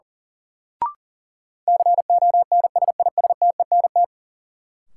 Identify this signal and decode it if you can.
Morse code, heard as a keyed tone, about 37 wpm, 700 Hz. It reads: CONSISTENT
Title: Sentences from Top 1500 Words - New Word Drill Alt - 40wpm